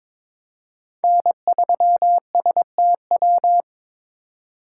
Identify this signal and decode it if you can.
Morse code, heard as a keyed tone, about 22 wpm, 700 Hz. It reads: N3STW
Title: USA Call Signs - 5 Characters - 3x Character Spacing - 22wpm